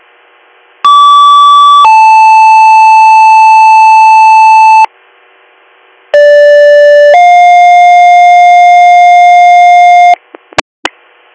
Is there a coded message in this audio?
Yes: two-tone page from 0.8 to 10.1 s, heard as a one-second tone, then a three-second tone, about 1150 then 880 Hz, about 580 then 710 Hz.